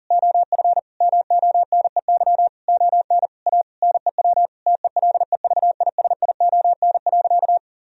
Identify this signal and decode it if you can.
Morse code, heard as a keyed tone, about 40 wpm, 700 Hz. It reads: OF MONEY ON A NEW TELEVISION.
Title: Sentences from Top 1000 Words - Review Part 4 - 40wpm